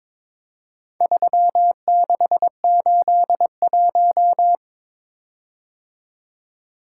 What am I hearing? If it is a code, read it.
Morse code, heard as a keyed tone, about 23 wpm, 700 Hz. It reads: 3681
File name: practice / 22wpm / Four Letter-Number Combinations - 22wpm